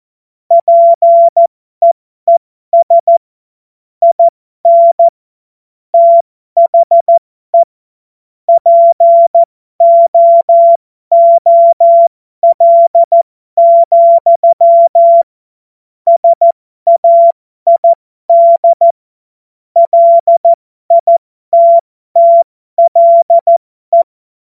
Morse code, heard as a keyed tone, about 14 wpm, spaced as for 12 wpm, 680 Hz. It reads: PEES IN THE POOL, SAID LITTLE
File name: Morsekurs / Witze